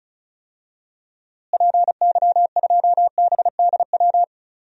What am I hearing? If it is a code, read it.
Morse code, heard as a keyed tone, about 36 wpm, 700 Hz. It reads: PY2BDW
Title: DXCC-100 Call Signs - 6 Characters - Repeated 3x - 35wpm